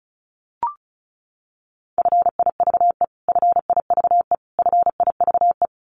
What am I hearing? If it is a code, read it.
Morse code, heard as a keyed tone, about 34 wpm, 700 Hz. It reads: FIVE FIVE FIVE
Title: Top 500 Words Repeated 35wpm